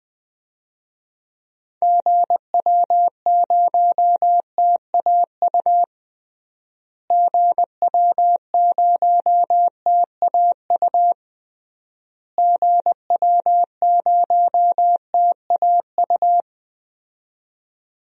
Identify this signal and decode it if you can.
Morse code, heard as a keyed tone, about 20 wpm, 700 Hz. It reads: GW0TAU GW0TAU GW0TAU
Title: DXCC-100 Call Signs - 6 Characters - Repeated 3x - 20wpm